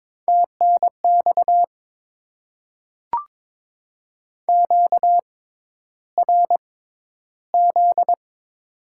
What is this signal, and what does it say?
Morse code, heard as a keyed tone, about 22 wpm, spaced as for 11 wpm, 700 Hz. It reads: TNX QRZ
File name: Top 90 QSO Elements - 6x Character Spacing - 22wpm